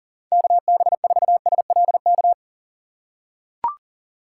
Morse code, heard as a keyed tone, about 40 wpm, 700 Hz. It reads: KB4SLK
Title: USA Call Signs - 6x Character Spacing - 40wpm